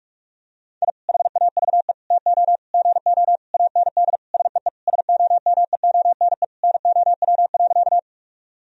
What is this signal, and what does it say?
Morse code, heard as a keyed tone, about 46 wpm, 700 Hz. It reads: I HAVE TO GO AND SEE SOMEONE NOW.
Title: Sentences from Top 1000 Words - Review Part 4 - 45wpm